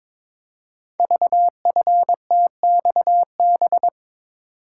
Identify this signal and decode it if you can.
Morse code, heard as a keyed tone, about 23 wpm, 700 Hz. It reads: VFTXB
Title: Code Groups - 5 Letters - 22wpm